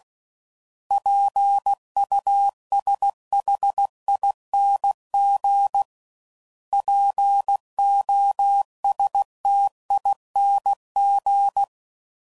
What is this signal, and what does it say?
Morse code, heard as a keyed tone, about 16 wpm, 800 Hz. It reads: PUSHING POSTING
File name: Suffixes / Special INGs / Sngl Ings